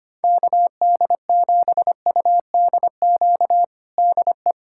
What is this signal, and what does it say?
Morse code, heard as a keyed tone, about 25 wpm, 700 Hz. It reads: KD7UDQ DE
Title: USA Callsign DE Callsign - 6 Characters - 25wpm